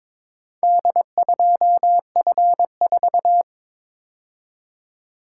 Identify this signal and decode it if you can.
Morse code, heard as a keyed tone, about 22 wpm, 700 Hz. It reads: D2F4
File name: practice / 22wpm / Four Letter-Number Combinations - 22wpm